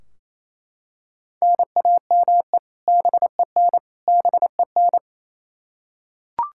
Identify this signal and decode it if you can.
Morse code, heard as a keyed tone, about 28 wpm, 700 Hz. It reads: NAME BEN BEN
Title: K1USN SST QSO Practice 28wpm